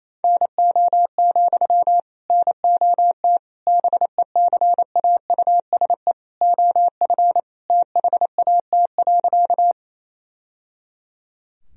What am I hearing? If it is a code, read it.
Morse code, heard as a keyed tone, about 28 wpm, 700 Hz. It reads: NO, NOT BECAUSE OF THAT.
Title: Sentences from Top 1000 Words - Review Part 3 - 28wpm